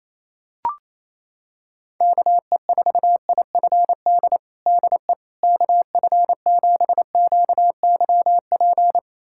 Morse code, heard as a keyed tone, about 28 wpm, 700 Hz. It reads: KE4IFD DE KF7QYP
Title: USA Callsign DE Callsign - 6 Characters - 28wpm